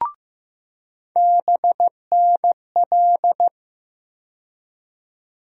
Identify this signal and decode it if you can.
Morse code, heard as a keyed tone, about 15 wpm, 700 Hz. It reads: BNL
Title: USA Call Sign Postfixes - 3 Letters 15wpm